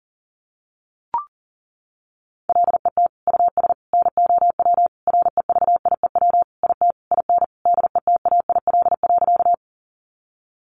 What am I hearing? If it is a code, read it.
Morse code, heard as a keyed tone, about 37 wpm, 700 Hz. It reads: LET US NOW REVIEW IT IN DETAIL.
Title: Sentences from Top 1500 Words 40wpm